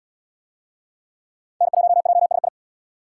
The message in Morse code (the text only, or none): N1JSI